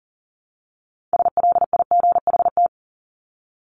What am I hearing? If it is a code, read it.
Morse code, heard as a keyed tone, about 40 wpm, 700 Hz. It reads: SLIGHT
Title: LY Suffix - Top 100 Words with Base Words 40wpm